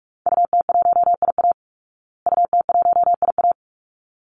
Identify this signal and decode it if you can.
Morse code, heard as a keyed tone, about 44 wpm, 700 Hz. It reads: UT1IA UT1IA
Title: Call Signs Repeated 3x - 45wpm